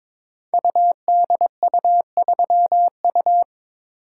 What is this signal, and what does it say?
Morse code, heard as a keyed tone, about 22 wpm, 690 Hz. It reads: UDU3U
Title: Code Groups - 5 Letter:Numbers - 22wpm